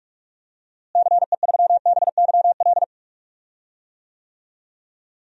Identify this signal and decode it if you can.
Morse code, heard as a keyed tone, about 46 wpm, 700 Hz. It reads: CE3BYL
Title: DXCC-100 Call Signs - 6 Characters - Repeated 3x - 45wpm